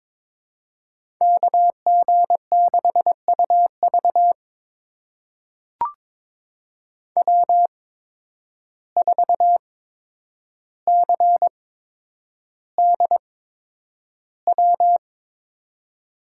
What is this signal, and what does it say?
Morse code, heard as a keyed tone, about 23 wpm, spaced as for 10 wpm, 700 Hz. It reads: KG6UV W4CDW
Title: USA Call Signs - 5 Characters - 8x Character Spacing - 22wpm